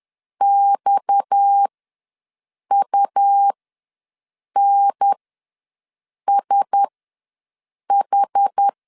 Morse code, heard as a keyed tone, about 10 wpm, spaced as for 6 wpm, 800 Hz. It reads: XUNSH
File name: LEZIONI / 7 GF9Z / 7 G ws 10wpm 3843